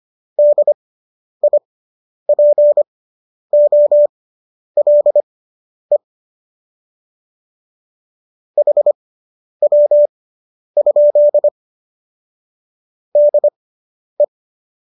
Morse code, heard as a keyed tone, about 25 wpm, spaced as for 10 wpm, 590 Hz. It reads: DIPOLE HW? DE